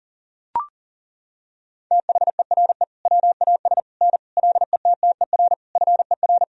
Morse code, heard as a keyed tone, about 40 wpm, 700 Hz. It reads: THERE WAS N LETTER FER
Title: Sentences from Top 300 Words and Abbreviations 40wpm